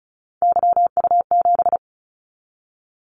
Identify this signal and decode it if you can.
Morse code, heard as a keyed tone, about 36 wpm, 700 Hz. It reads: YU7
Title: Callsign Prefixes for Top 25 DXCC - 35wpm